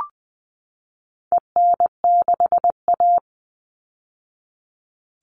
Morse code, heard as a keyed tone, about 20 wpm, 700 Hz. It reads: EN6A
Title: All Cut Numbers - 4-Digit 20wpm